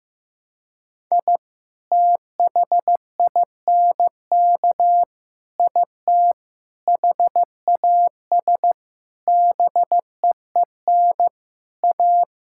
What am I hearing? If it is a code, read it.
Morse code, heard as a keyed tone, about 15 wpm, 700 Hz. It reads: I THINK IT HAS BEEN A
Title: Sentences from Top 1000 Words - Review Part 6 - 15wpm